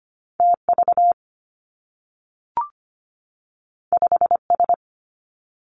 Morse code, heard as a keyed tone, about 25 wpm, 700 Hz. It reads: T V 5 S
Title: Two Challenging Letter-Number Combinations 25wpm